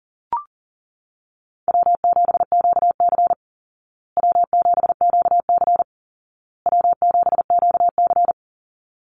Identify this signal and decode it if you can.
Morse code, heard as a keyed tone, about 37 wpm, 700 Hz. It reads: W7QC W7QC W7QC
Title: Easy Call Signs Repeated 3x - 40wpm